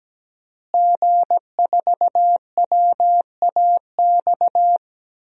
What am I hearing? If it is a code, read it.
Morse code, heard as a keyed tone, about 17 wpm, 700 Hz. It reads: G4WAX
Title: DXCC-100 Call Signs - 5 Characters - Repeated 3x - 17wpm